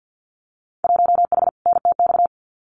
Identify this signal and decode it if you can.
Morse code, heard as a keyed tone, about 50 wpm, 700 Hz. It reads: 1H NTX
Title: Winter Field Day Exchanges 50wpm